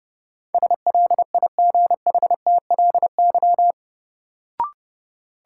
Morse code, heard as a keyed tone, about 30 wpm, 700 Hz. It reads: SLIGHTLY
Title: Sentences from Top 1300 Words - New Word Drill Alt - 30wpm